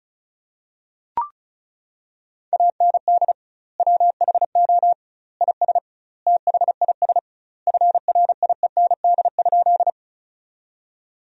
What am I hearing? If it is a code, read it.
Morse code, heard as a keyed tone, about 36 wpm, 700 Hz. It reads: AND WHO IS THIS FRIEND?
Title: 2x Word-Spacing - Sentences from Top 300 Words - 35wpm